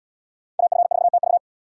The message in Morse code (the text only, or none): AF4TV